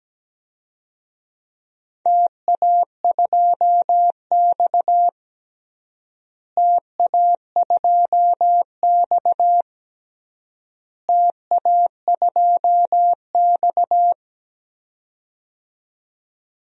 Morse code, heard as a keyed tone, about 17 wpm, 700 Hz. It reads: TA2X TA2X TA2X
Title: DXCC-100 Call Signs - 4 Characters - Repeated 3x - 17wpm